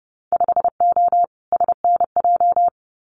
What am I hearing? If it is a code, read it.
Morse code, heard as a keyed tone, about 30 wpm, 700 Hz. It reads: 5O SNJ
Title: Winter Field Day Exchanges Encore 30wpm